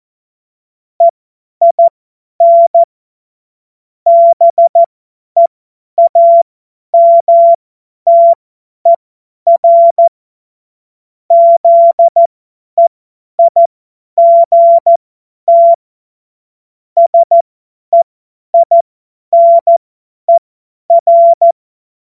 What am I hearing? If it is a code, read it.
Morse code, heard as a keyed tone, about 14 wpm, spaced as for 10 wpm, 680 Hz. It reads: EIN BEAMTER ZEIGT SEINER
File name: Witz-07-Oktober-2025-50-F70